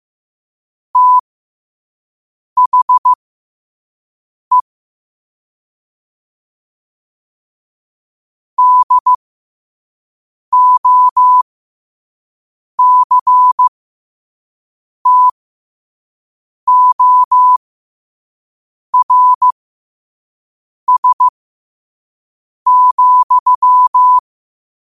Morse code, heard as a keyed tone, about 15 wpm, spaced as for 5 wpm, 1000 Hz. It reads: THE DOCTORS,
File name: qotdfiles / 5wpm